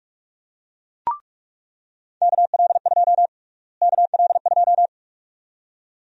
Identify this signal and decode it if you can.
Morse code, heard as a keyed tone, about 46 wpm, 700 Hz. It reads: KL2 KL2
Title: USA Call Sign Prefixes - Repeated 2x - 45wpm